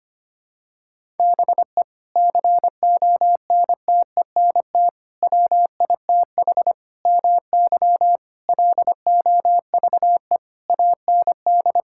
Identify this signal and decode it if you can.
Morse code, heard as a keyed tone, about 25 wpm, 700 Hz. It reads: BE CONTENT WITH MY LOVE AND
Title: Sentences from Top 1000 Words - Review Part 6 - 25wpm